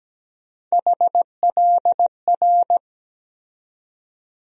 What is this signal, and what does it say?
Morse code, heard as a keyed tone, about 17 wpm, 700 Hz. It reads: HLR
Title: USA Call Sign Postfixes - 3 Letters 17wpm